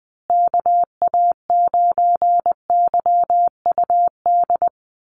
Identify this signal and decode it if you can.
Morse code, heard as a keyed tone, about 20 wpm, 700 Hz. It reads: KA9YUD
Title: USA Call Signs Repeated 3x 20wpm